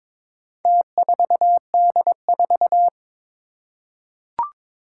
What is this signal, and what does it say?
Morse code, heard as a keyed tone, about 22 wpm, 700 Hz. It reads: T4D4
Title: Four Challenging Letter-Number Combinations 22wpm